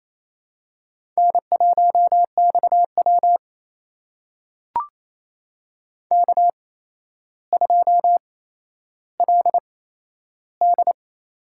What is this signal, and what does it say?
Morse code, heard as a keyed tone, about 28 wpm, spaced as for 12 wpm, 700 Hz. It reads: N1XW K2LD
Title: USA Call Signs - 4 Characters - 8x Character Spacing - 28wpm